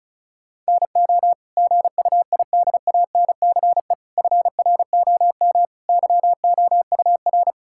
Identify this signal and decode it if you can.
Morse code, heard as a keyed tone, about 34 wpm, 700 Hz. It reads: NO GUIDANCE FROM YOUR
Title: Sentences from Top 1600 Words 35wpm